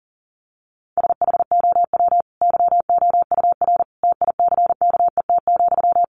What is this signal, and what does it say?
Morse code, heard as a keyed tone, about 40 wpm, 700 Hz. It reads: SHOW YOUR TICKET,
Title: Sentences from Top 1400 Words 40wpm